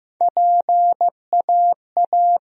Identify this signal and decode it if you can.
Morse code, heard as a keyed tone, about 15 wpm, 700 Hz. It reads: PAA
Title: USA Call Sign Postfixes - 3 Letters 15wpm